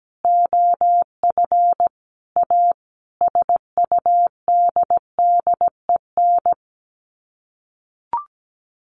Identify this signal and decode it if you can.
Morse code, heard as a keyed tone, about 17 wpm, 700 Hz. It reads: OF A SUDDEN
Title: Sets of 5 Words 17wpm